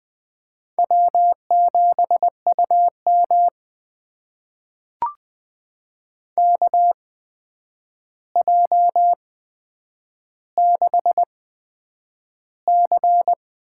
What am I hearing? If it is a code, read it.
Morse code, heard as a keyed tone, about 20 wpm, spaced as for 9 wpm, 700 Hz. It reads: W7UM KJ6C
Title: USA Call Signs - 8x Character Spacing - 20wpm